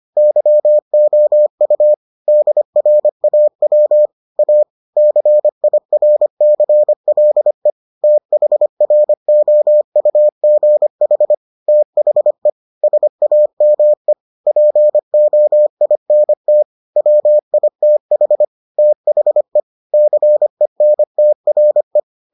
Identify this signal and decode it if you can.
Morse code, heard as a keyed tone, about 25 wpm, 600 Hz. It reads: YOU DRAW A CIRCLE THROUGH THE SAME POINT WITH THE CENTRE